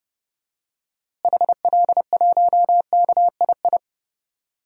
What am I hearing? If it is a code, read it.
Morse code, heard as a keyed tone, about 30 wpm, 700 Hz. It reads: HL1KII
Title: DXCC-100 Call Signs - 6 Characters - Repeated 2x - 30wpm